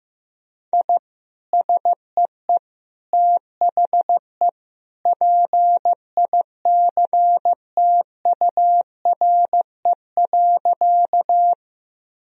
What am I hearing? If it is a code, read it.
Morse code, heard as a keyed tone, about 15 wpm, 700 Hz. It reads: I SEE THE PICTURE.
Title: Sentences from Top 1000 Words - Review Part 1 - 15wpm